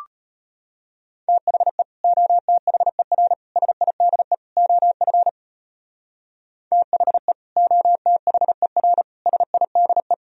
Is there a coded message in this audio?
Yes: Morse code from 1.3 to 10.1 s, heard as a keyed tone, about 33 wpm, 700 Hz. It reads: THE OTHER SIDE OF THE OTHER SIDE